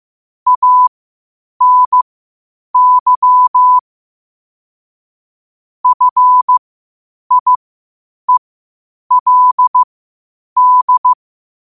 Morse code, heard as a keyed tone, about 15 wpm, spaced as for 8 wpm, 1000 Hz. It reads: ANY FIELD
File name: qotdfiles / 8wpm